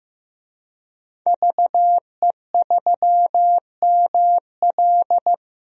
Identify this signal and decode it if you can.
Morse code, heard as a keyed tone, about 15 wpm, 700 Hz. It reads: VE3ML
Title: DXCC-100 Call Signs - 5 Characters - Repeated 3x - 15wpm